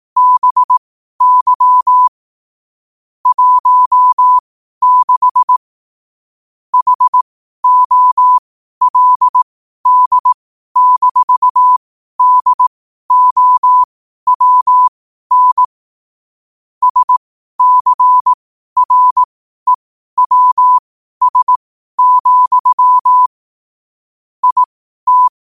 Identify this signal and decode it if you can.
Morse code, heard as a keyed tone, about 18 wpm, spaced as for 12 wpm, 1000 Hz. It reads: BY 16 HOLD-DOWN SCREWS, IT
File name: qotdfiles / 12wpm